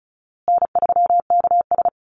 Morse code, heard as a keyed tone, about 34 wpm, 700 Hz. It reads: N3KS
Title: Easy Call Signs Repeated 3x - 35wpm